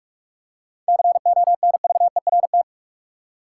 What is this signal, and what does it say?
Morse code, heard as a keyed tone, about 46 wpm, 700 Hz. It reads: CONVERT
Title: Seven Letter Words - 45wpm